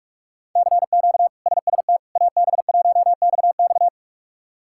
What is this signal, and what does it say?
Morse code, heard as a keyed tone, about 41 wpm, 700 Hz. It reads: CQ SST AB1XX
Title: K1USN SST QSO Practice - 45wpm